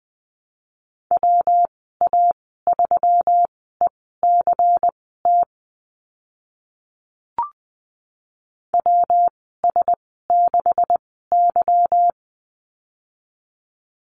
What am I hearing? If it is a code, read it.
Morse code, heard as a keyed tone, about 20 wpm, spaced as for 15 wpm, 700 Hz. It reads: WA3ECT WS6Y
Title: 147 Lesson 24g - Letter y - Call Signs 20wpm